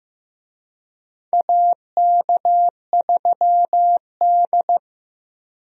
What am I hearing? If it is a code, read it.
Morse code, heard as a keyed tone, about 15 wpm, 700 Hz. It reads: AK3D
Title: USA Call Signs - 4 Characters - 15wpm